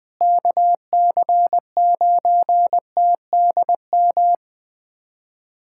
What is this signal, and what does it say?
Morse code, heard as a keyed tone, about 20 wpm, 700 Hz. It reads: KC9TDM
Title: USA Callsign DE Callsign - 6 Characters - 20wpm